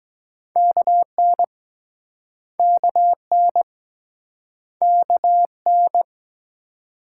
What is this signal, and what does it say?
Morse code, heard as a keyed tone, about 22 wpm, 700 Hz. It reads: KN KN KN